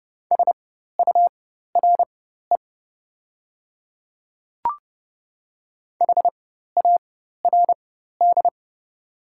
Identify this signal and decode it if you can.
Morse code, heard as a keyed tone, about 30 wpm, spaced as for 14 wpm, 700 Hz. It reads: SURE HARD
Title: 4x Character Spacing - Top 300 Words - 30wpm